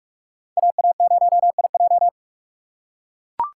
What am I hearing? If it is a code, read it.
Morse code, heard as a keyed tone, about 43 wpm, 700 Hz. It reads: AA0IJ